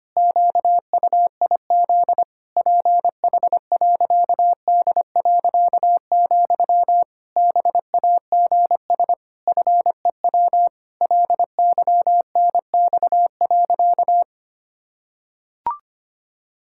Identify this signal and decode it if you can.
Morse code, heard as a keyed tone, about 25 wpm, 700 Hz. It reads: QUIZ PH.D., BAGS FEW LYNX.
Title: Pangram Challenge v2 - 25wpm